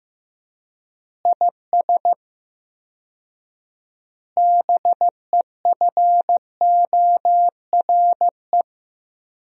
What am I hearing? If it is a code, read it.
Morse code, heard as a keyed tone, about 15 wpm, 700 Hz. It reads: IS BEFORE
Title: 4x Word-Spacing - Sentences from Top 200 Words - 15wpm